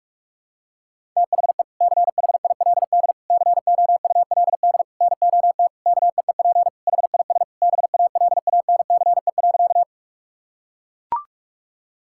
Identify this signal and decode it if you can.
Morse code, heard as a keyed tone, about 42 wpm, 700 Hz. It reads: THE CHILD COULD NOT KEEP HIS BALANCE.
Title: Sentences from Top 1000 Words - Review Part 6 - 45wpm